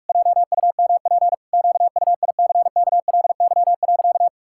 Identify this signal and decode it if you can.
Morse code, heard as a keyed tone, about 46 wpm, 700 Hz. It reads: JUMP QUICKLY.
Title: Pangram Challenge v2 - 45wpm